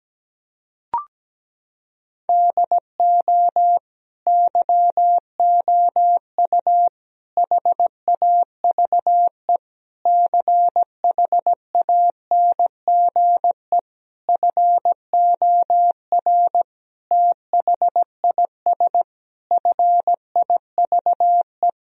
Morse code, heard as a keyed tone, about 17 wpm, 700 Hz. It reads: DO YOU HAVE CHANGE FOR THIS FIVE
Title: Sentences from Top 1000 Words - Review Part 2 - 17wpm